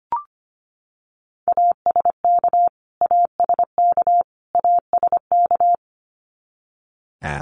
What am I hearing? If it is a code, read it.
Morse code, heard as a keyed tone, about 25 wpm, 700 Hz. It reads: ASK ASK ASK
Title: Top 200 Words Repeated 25wpm